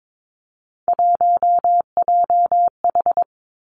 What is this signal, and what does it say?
Morse code, heard as a keyed tone, about 22 wpm, 700 Hz. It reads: 1JH
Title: Three Challenging Letter-Number Combinations 22wpm